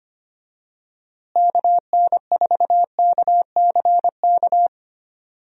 Morse code, heard as a keyed tone, about 25 wpm, 700 Hz. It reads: KN4KCK
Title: USA Call Signs - 6 Characters - Repeated 2x - 25wpm